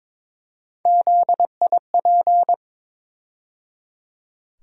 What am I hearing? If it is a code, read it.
Morse code, heard as a keyed tone, about 21 wpm, 700 Hz. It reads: ZIP
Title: USA Call Sign Postfixes - 3 Letters 22wpm